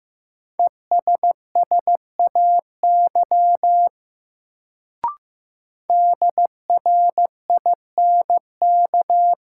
Morse code, heard as a keyed tone, about 15 wpm, 700 Hz. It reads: ESSAY DRINK
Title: Sentences from Top 1500 Words - New Word Drill - 15wpm